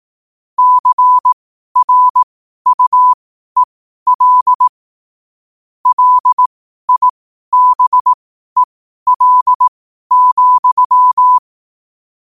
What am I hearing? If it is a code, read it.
Morse code, heard as a keyed tone, about 18 wpm, spaced as for 12 wpm, 1000 Hz. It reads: CRUEL LIBEL,
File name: qotdfiles / 12wpm